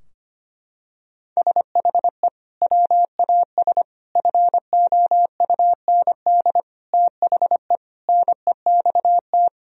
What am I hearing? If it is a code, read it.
Morse code, heard as a keyed tone, about 25 wpm, 700 Hz. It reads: SHE WAS FOUND THE NEXT
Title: Sentences from Top 1000 Words - Review Part 3 - 25wpm